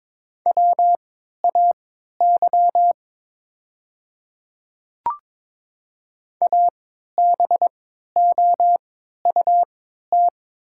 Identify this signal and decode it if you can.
Morse code, heard as a keyed tone, about 22 wpm, spaced as for 13 wpm, 700 Hz. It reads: WAY ABOUT
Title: 3x Character Spacing - Top 200 Words - 22wpm